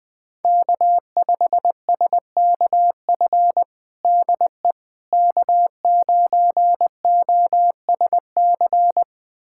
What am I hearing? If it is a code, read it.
Morse code, heard as a keyed tone, about 21 wpm, 700 Hz. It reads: K5SKF DE K9OSC
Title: USA Callsign DE Callsign - 5 Characters - 20wpm